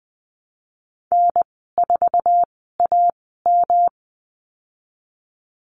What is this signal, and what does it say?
Morse code, heard as a keyed tone, about 20 wpm, spaced as for 15 wpm, 700 Hz. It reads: N4AM
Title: 140 Lesson 23g - Letter f - Call Signs 20wpm